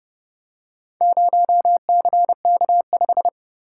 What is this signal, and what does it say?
Morse code, heard as a keyed tone, about 28 wpm, 700 Hz. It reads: 0CK5